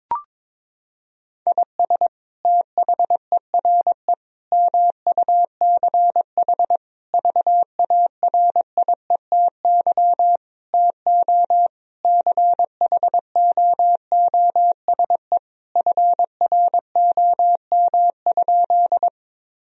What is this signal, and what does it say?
Morse code, heard as a keyed tone, about 22 wpm, 700 Hz. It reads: IS THERE MUCH VARIETY TO CHOOSE FROM?
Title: Sentences from Top 1000 Words - Review Part 4 - 22wpm